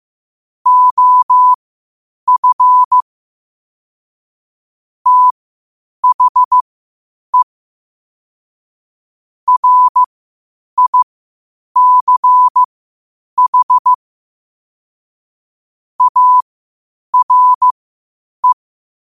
Morse code, heard as a keyed tone, about 15 wpm, spaced as for 8 wpm, 1000 Hz. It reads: OF THE RICH ARE
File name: qotdfiles / 8wpm